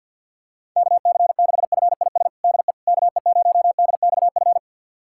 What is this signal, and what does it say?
Morse code, heard as a keyed tone, about 49 wpm, 700 Hz. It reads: KC6FIS DE KE0DXF
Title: USA Callsign DE Callsign - 6 Characters - 50wpm